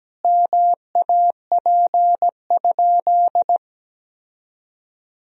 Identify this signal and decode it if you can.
Morse code, heard as a keyed tone, about 17 wpm, 700 Hz. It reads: MAP?
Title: Sentences from Top 1000 Words - Review Part 2 - 17wpm